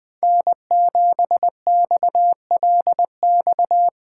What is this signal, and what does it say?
Morse code, heard as a keyed tone, about 20 wpm, 700 Hz. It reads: N7XLX
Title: USA Call Signs - 5 Characters - 2x Character Spacing - 20wpm